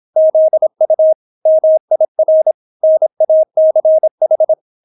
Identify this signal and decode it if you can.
Morse code, heard as a keyed tone, about 26 wpm, 630 Hz. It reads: ZU MIR NACH